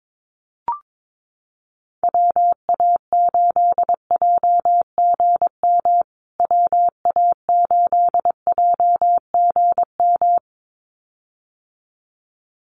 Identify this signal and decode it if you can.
Morse code, heard as a keyed tone, about 22 wpm, 700 Hz. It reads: WA8JGM WA8JGM
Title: USA Call Signs Repeated 2x 22wpm